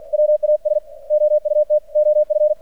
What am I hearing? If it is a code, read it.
Morse code, heard as a keyed tone, about 48 wpm, 600 Hz. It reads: WAS OUT OF
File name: archived / CRSnet